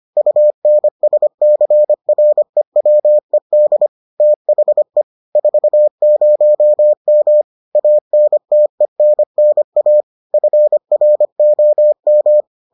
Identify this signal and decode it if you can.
Morse code, heard as a keyed tone, about 25 wpm, 600 Hz. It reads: UNSCREWED THE 40M ANTENNA FROM